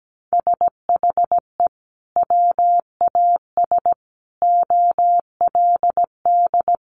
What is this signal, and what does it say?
Morse code, heard as a keyed tone, about 17 wpm, 700 Hz. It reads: SHE WAS OLD
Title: Sentences from Top 800 Words 17wpm